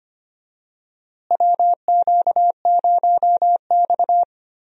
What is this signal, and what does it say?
Morse code, heard as a keyed tone, about 25 wpm, 700 Hz. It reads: WQ0X
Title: USA Call Signs - 4 Characters - Repeated 3x - 25wpm